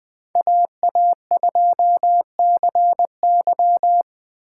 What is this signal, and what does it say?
Morse code, heard as a keyed tone, about 20 wpm, 700 Hz. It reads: AA2CY